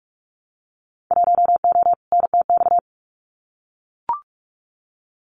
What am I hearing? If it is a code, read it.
Morse code, heard as a keyed tone, about 44 wpm, 700 Hz. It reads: 1O NTX
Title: Winter Field Day Exchanges 45wpm